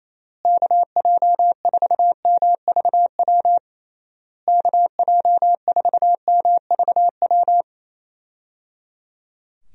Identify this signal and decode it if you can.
Morse code, heard as a keyed tone, about 27 wpm, 700 Hz. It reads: KJ4MVW KJ4MVW
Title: USA Call Signs - 6 Characters - Repeated 3x - 28wpm